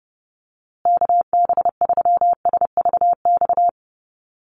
Morse code, heard as a keyed tone, about 30 wpm, 700 Hz. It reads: KB3SVX
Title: USA Call Signs Repeated 2x 30wpm